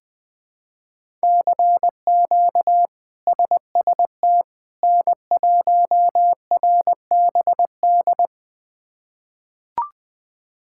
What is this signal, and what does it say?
Morse code, heard as a keyed tone, about 20 wpm, 700 Hz. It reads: CQ SST N1RBD
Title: K1USN SST QSO Practice - 20wpm